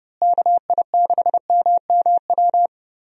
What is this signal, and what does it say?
Morse code, heard as a keyed tone, about 32 wpm, 700 Hz. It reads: KI6MMW
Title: USA Call Signs - 6 Characters - Repeated 2x - 30wpm